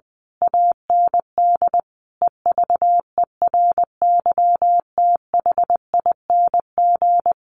Morse code, heard as a keyed tone, about 20 wpm, 700 Hz. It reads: AND EVERYTHING
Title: A Sound Of Thunder 20wpm